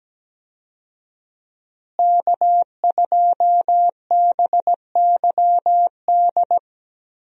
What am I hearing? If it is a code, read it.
Morse code, heard as a keyed tone, about 17 wpm, 700 Hz. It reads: K2BYD
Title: USA Call Signs - 5 Characters - Repeated 2x - 17wpm